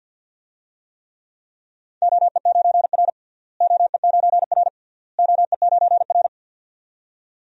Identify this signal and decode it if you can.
Morse code, heard as a keyed tone, about 52 wpm, 700 Hz. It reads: OE9R OE9R OE9R
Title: DXCC-100 Call Signs - 4 Characters - Repeated 3x - 50wpm